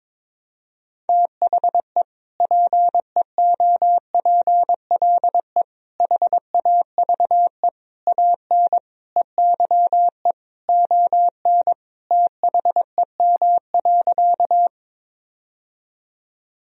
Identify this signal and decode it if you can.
Morse code, heard as a keyed tone, about 22 wpm, 700 Hz. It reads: THE PEOPLE HAVE AN EYE ON THEM.
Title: Sentences from Top 1000 Words - Review Part 1 - 22wpm